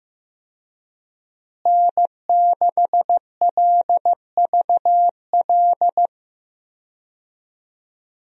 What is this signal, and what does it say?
Morse code, heard as a keyed tone, about 16 wpm, 700 Hz. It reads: N6LVL
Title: USA Call Signs - 5 Characters - Repeated 2x - 15wpm